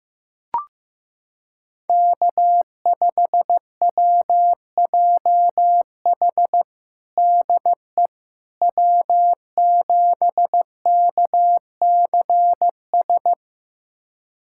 Morse code, heard as a keyed tone, about 15 wpm, 700 Hz. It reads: K5WJH DE W7KCS
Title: USA Callsign DE Callsign - 5 Characters - 15wpm